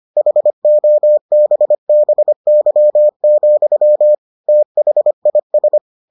Morse code, heard as a keyed tone, about 25 wpm, 600 Hz. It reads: HOBBY, THIS